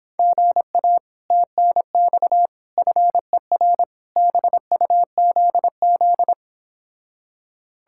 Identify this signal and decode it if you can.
Morse code, heard as a keyed tone, about 31 wpm, 700 Hz. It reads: GA TNX FER BUZZ